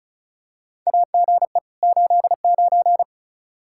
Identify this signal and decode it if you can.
Morse code, heard as a keyed tone, about 34 wpm, 700 Hz. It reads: AGE 89
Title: K1USN SST QSO Practice 35wpm